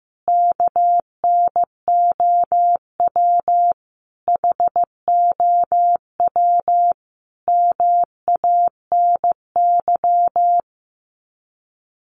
Morse code, heard as a keyed tone, about 15 wpm, 700 Hz. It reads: KNOW HOW MANY
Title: Sets of 5 Words Encore 15wpm